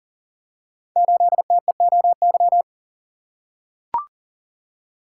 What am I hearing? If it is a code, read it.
Morse code, heard as a keyed tone, about 39 wpm, 690 Hz. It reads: 8TEOY
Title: Code Groups - 5 Letter:Numbers - 40wpm